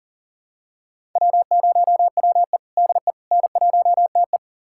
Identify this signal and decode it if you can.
Morse code, heard as a keyed tone, about 40 wpm, 700 Hz. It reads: W0WE DE N1TE
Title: USA Callsign DE Callsign - 4 Characters - 40wpm